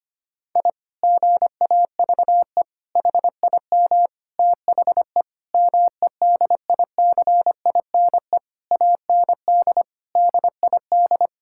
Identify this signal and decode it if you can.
Morse code, heard as a keyed tone, about 25 wpm, 700 Hz. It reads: I GAVE HIM THE MEDICINE AND DID
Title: Sentences from Top 1000 Words - Review Part 6 - 25wpm